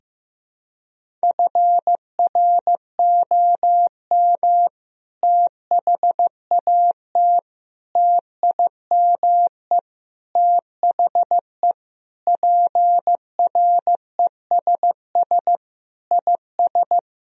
Morse code, heard as a keyed tone, about 15 wpm, 700 Hz. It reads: FROM THAT TIME THE PRESS IS
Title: Sentences from Top 1000 Words - Review Part 1 - 15wpm